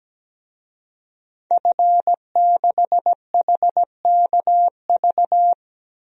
Morse code, heard as a keyed tone, about 17 wpm, 700 Hz. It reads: F6HKV
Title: DXCC-100 Call Signs - 5 Characters - 17wpm